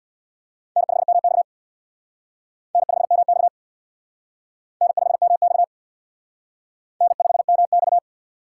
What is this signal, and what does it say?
Morse code, heard as a keyed tone, about 62 wpm, 690 Hz. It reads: N5MX N5MX N5MX N5MX
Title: Speed Racing - USA 1x2 Call Signs 50wpm